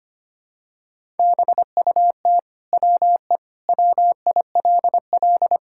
Morse code, heard as a keyed tone, about 25 wpm, 700 Hz. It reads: BUT WE WILL
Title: Sentences from Top 1000 Words - Review Part 5 - 25wpm